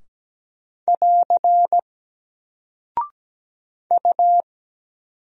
Morse code, heard as a keyed tone, about 17 wpm, 700 Hz. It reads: <AR> U
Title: Single Character (Letter:Number:Punctuation:Prosign) 17wpm